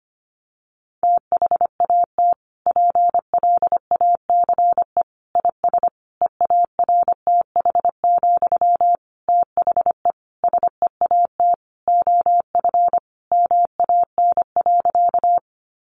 Morse code, heard as a keyed tone, about 25 wpm, 700 Hz. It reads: THAT PLACE IS EARTH, THE SEAT OF MAN.
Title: Sentences from Top 1500 Words 25wpm